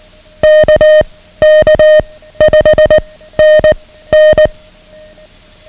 Morse code, heard as a keyed tone, about 19 wpm, 620 Hz. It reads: K K 5 N N
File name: ham radio / CodeQuick / Code Quick 09